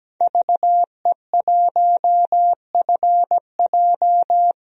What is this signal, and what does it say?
Morse code, heard as a keyed tone, about 17 wpm, 700 Hz. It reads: VE1FJ
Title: DXCC-100 Call Signs - 5 Characters - 17wpm